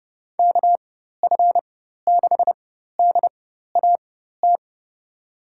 Morse code, heard as a keyed tone, about 30 wpm, spaced as for 14 wpm, 700 Hz. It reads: KF6DAT
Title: USA Call Signs - 4x Character Spacing - 30wpm